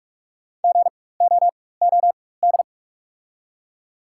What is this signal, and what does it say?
Morse code, heard as a keyed tone, about 48 wpm, spaced as for 21 wpm, 700 Hz. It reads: GOOD
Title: 4x Character Spacing - Top 500 Words - 45wpm